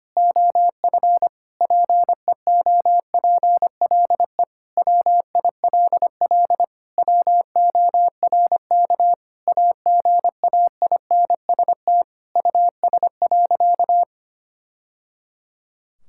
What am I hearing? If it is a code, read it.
Morse code, heard as a keyed tone, about 25 wpm, 700 Hz. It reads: OF PEOPLE WILL WORK AGAINST US.
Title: Sentences from Top 1000 Words - Review Part 6 - 25wpm